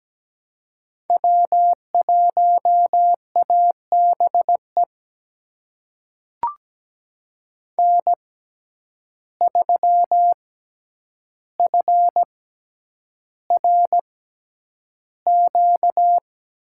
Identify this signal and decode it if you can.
Morse code, heard as a keyed tone, about 17 wpm, spaced as for 9 wpm, 700 Hz. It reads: W1ABE N3FRQ
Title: USA Call Signs - 6x Character Spacing - 17wpm